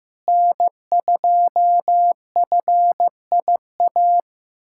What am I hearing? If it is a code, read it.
Morse code, heard as a keyed tone, about 15 wpm, 700 Hz. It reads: N2FIA
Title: USA Callsign DE Callsign - 5 Characters - 15wpm